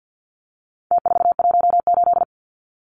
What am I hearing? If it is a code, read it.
Morse code, heard as a keyed tone, about 50 wpm, 700 Hz. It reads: T418
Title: Numbers - Contest Style - 4-Digit 50wpm